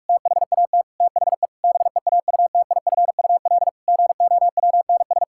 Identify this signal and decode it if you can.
Morse code, heard as a keyed tone, about 44 wpm, 700 Hz. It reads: THAT THE BEAUTIFUL GOWNS
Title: Pangram Challenge v2 - 45wpm